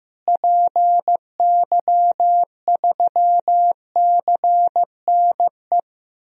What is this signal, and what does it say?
Morse code, heard as a keyed tone, about 16 wpm, 700 Hz. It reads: PY3CNE
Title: DXCC-100 Call Signs - 6 Characters - Repeated 3x - 15wpm